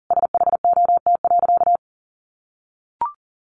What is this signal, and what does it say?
Morse code, heard as a keyed tone, about 40 wpm, 700 Hz. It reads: SHOT.
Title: A Sound Of Thunder 40wpm